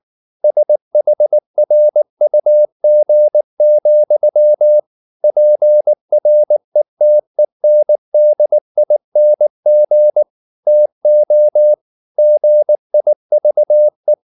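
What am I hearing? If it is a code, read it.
Morse code, heard as a keyed tone, about 19 wpm, 600 Hz. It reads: SHRUG, PRETENDING TO GIVE
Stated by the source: Morse code at 19 wpm from CW med Gnister, reading Ken Follett's novel Never in English.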